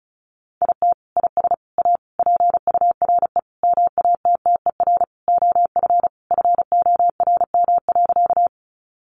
Morse code, heard as a keyed tone, about 34 wpm, 700 Hz. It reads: IT IS A PURE MATTER OF FORM.
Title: Sentences from Top 1500 Words 35wpm